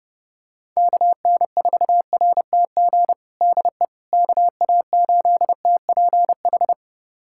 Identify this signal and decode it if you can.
Morse code, heard as a keyed tone, about 30 wpm, 700 Hz. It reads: KN4RTG DE KA8TPH
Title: USA Callsign DE Callsign - 6 Characters - 30wpm